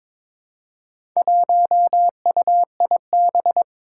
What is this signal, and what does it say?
Morse code, heard as a keyed tone, about 22 wpm, 700 Hz. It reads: 1UIB
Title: Four Challenging Letter-Number Combinations 22wpm